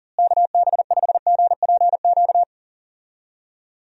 Morse code, heard as a keyed tone, about 40 wpm, 700 Hz. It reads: KB5GPQ
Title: USA Call Signs - 6 Characters - Repeated 3x - 40wpm